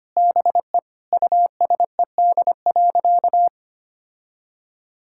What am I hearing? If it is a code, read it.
Morse code, heard as a keyed tone, about 25 wpm, 700 Hz. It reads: BE USED.
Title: Sentences from Top 1600 Words 25wpm